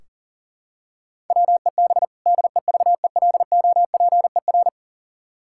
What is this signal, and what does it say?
Morse code, heard as a keyed tone, about 40 wpm, 700 Hz. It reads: WEB DEVELOPER
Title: K1USN SST QSO Practice 40wpm